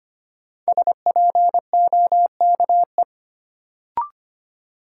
Morse code, heard as a keyed tone, about 25 wpm, 700 Hz. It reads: SPOKE
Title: Sentences from Top 1200 Words - New Word Drill - 25wpm